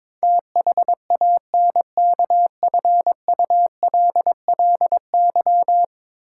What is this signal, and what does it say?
Morse code, heard as a keyed tone, about 22 wpm, 700 Hz. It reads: THANKFULLY
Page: morse-code-ninja-practice-22wpm